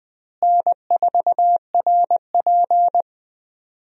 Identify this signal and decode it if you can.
Morse code, heard as a keyed tone, about 20 wpm, 700 Hz. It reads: N4RP
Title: USA Call Signs - 4 Characters - Repeated 3x - 20wpm